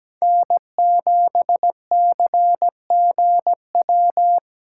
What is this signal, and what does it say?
Morse code, heard as a keyed tone, about 20 wpm, 700 Hz. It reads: N7CGW
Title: Speed Racing - USA 1x3 Call Signs 17wpm